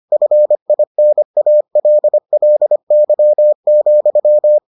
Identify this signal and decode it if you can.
Morse code, heard as a keyed tone, about 25 wpm, 600 Hz. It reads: FINALLY,